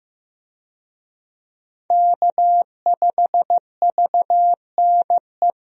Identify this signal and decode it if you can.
Morse code, heard as a keyed tone, about 16 wpm, 700 Hz. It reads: K5VNE
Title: USA Call Signs - 5 Characters - Repeated 2x - 15wpm